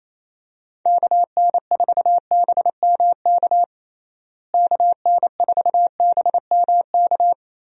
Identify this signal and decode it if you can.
Morse code, heard as a keyed tone, about 28 wpm, 700 Hz. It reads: KN4BMK KN4BMK
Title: USA Call Signs - 6 Characters - Repeated 3x - 28wpm